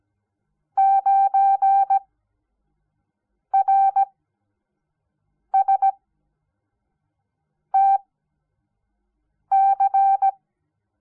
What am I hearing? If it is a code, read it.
Morse code, heard as a keyed tone, about 17 wpm, spaced as for 5 wpm, 780 Hz. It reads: 9RSTC
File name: Onlinelehrgang / morsen / DJ4UF Morsen06